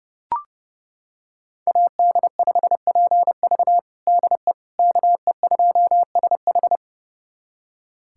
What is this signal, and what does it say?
Morse code, heard as a keyed tone, about 31 wpm, 700 Hz. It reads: AD5PV DE KE2SH
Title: USA Callsign DE Callsign - 5 Characters - 30wpm